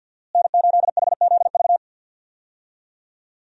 Morse code, heard as a keyed tone, about 47 wpm, 700 Hz. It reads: N8HZV
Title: USA Call Signs - 5 Characters - Repeated 3x - 50wpm